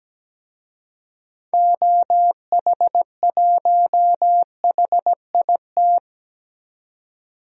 Morse code, heard as a keyed tone, about 17 wpm, 700 Hz. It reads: OH1HIT
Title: DXCC-100 Call Signs - 6 Characters - 17wpm